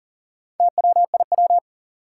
Morse code, heard as a keyed tone, about 38 wpm, 700 Hz. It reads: TWIW
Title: Four Challenging Letter-Number Combinations 40wpm